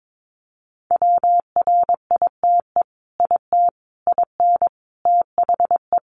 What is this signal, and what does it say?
Morse code, heard as a keyed tone, about 22 wpm, 700 Hz. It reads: WRITE IT IN THE
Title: Sentences from Top 600 Words 22wpm